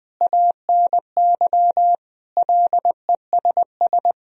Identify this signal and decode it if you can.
Morse code, heard as a keyed tone, about 21 wpm, 700 Hz. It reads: ANY LESS
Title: Sentences from Top 1000 Words - Review Part 2 - 20wpm